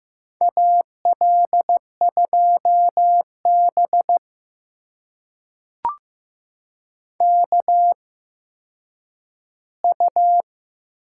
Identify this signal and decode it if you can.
Morse code, heard as a keyed tone, about 15 wpm, spaced as for 9 wpm, 700 Hz. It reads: AL2B KU
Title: USA Call Signs - 4 Characters - 8x Character Spacing - 15wpm